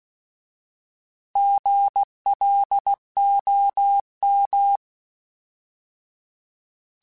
Morse code, heard as a keyed tone, about 16 wpm, 800 Hz. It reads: GLOM
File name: Alphabetical / Alphabetical mp3s / Gs @15